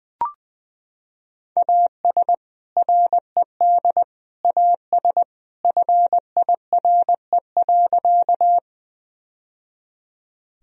Morse code, heard as a keyed tone, about 20 wpm, 700 Hz. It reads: AS RED AS FIRE.
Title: Sentences from Top 1000 Words - Review Part 2 - 20wpm